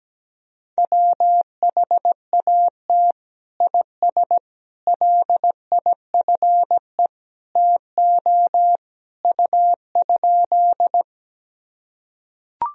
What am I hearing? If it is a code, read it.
Morse code, heard as a keyed tone, about 17 wpm, 700 Hz. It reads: WHAT IS LIFE TO U?
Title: Sentences from Top 300 Words and Abbreviations 17wpm